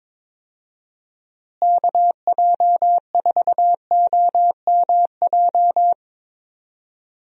Morse code, heard as a keyed tone, about 21 wpm, 700 Hz. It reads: KJ4OMJ
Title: USA Call Signs - 6 Characters - Repeated 2x - 22wpm